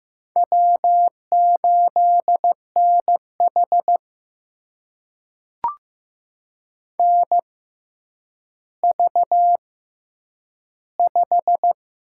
Morse code, heard as a keyed tone, about 15 wpm, spaced as for 9 wpm, 700 Hz. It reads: W8NH NV5
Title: USA Call Signs - 4 Characters - 6x Character Spacing - 15wpm